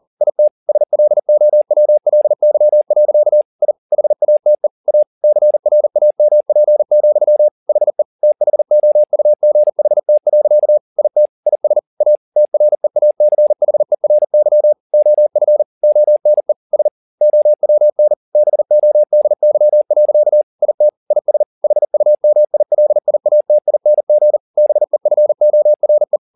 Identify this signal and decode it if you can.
Morse code, heard as a keyed tone, about 40 wpm, 600 Hz. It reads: IT SLOWLY. I HATE A CRAMP, HE THOUGHT. IT IS A TREACHERY OF ONE S OWN BODY. IT IS HUMILIATING BEFORE